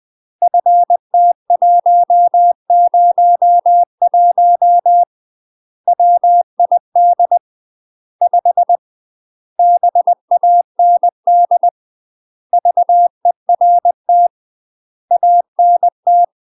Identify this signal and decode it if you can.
Morse code, heard as a keyed tone, about 20 wpm, 700 Hz. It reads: FT101 WID 5 BAND VERT ANT
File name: UK MASTER QSOs TRAINING / 20WPM / UK MASTER QSOs 0023